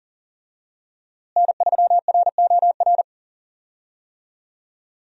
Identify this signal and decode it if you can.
Morse code, heard as a keyed tone, about 40 wpm, 700 Hz. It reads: N3ROR
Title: USA Call Signs - 5 Characters - Repeated 2x - 40wpm